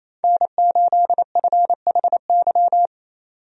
Morse code, heard as a keyed tone, about 27 wpm, 700 Hz. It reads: N8FHY
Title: USA Call Signs - 5 Characters - 28wpm